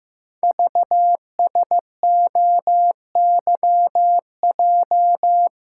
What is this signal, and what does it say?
Morse code, heard as a keyed tone, about 15 wpm, 700 Hz. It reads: VSOYJ
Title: Code Groups - 5 Letters - 15wpm